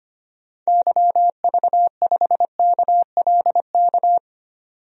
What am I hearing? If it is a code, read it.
Morse code, heard as a keyed tone, about 25 wpm, 700 Hz. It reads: YV5KLK
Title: DXCC-100 Call Signs - 6 Characters - 25wpm